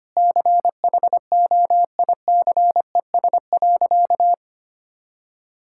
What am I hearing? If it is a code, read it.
Morse code, heard as a keyed tone, about 25 wpm, 700 Hz. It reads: CHOICES.
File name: Sentences with 3-Word Phrases - 25wpm